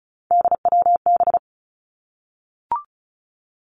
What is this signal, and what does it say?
Morse code, heard as a keyed tone, about 36 wpm, 700 Hz. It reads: DWB
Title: Three Challenging Letter-Number Combinations 35wpm